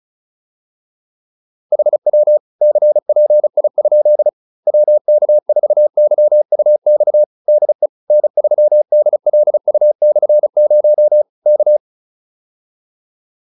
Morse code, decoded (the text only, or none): HW CPI? WK4YUX DE N3DLU/0 K